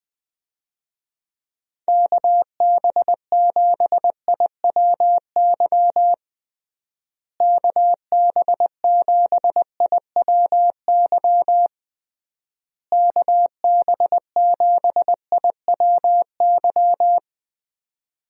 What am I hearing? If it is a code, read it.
Morse code, heard as a keyed tone, about 21 wpm, 700 Hz. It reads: KB7IWY KB7IWY KB7IWY
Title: USA Call Signs - 6 Characters - Repeated 3x - 20wpm